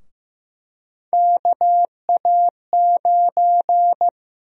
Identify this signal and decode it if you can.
Morse code, heard as a keyed tone, about 15 wpm, 700 Hz. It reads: KA9
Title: USA Call Sign Prefixes - 15wpm